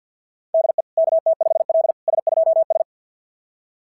Morse code, heard as a keyed tone, about 48 wpm, 690 Hz. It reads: DE KT5L S2S
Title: SOTA QSO Practice 50wpm